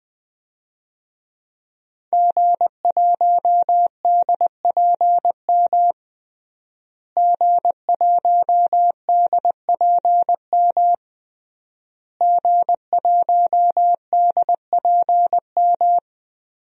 Morse code, heard as a keyed tone, about 20 wpm, 700 Hz. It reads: G1DPM G1DPM G1DPM
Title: DXCC-100 Call Signs - 5 Characters - Repeated 3x - 20wpm